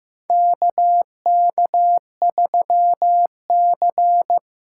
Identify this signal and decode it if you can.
Morse code, heard as a keyed tone, about 16 wpm, 700 Hz. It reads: KK3C
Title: USA Callsign DE Callsign - 4 Characters - 15wpm